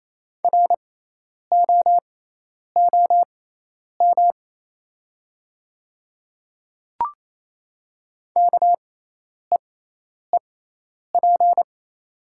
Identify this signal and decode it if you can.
Morse code, heard as a keyed tone, about 28 wpm, spaced as for 10 wpm, 700 Hz. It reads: ROOM KEEP
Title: 6x Character Spacing - Top 300 Words - 28wpm